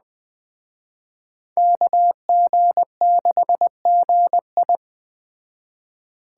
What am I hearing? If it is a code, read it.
Morse code, heard as a keyed tone, about 21 wpm, 700 Hz. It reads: KG6GI
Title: Speed Racing - USA 2x2 Call Signs 17wpm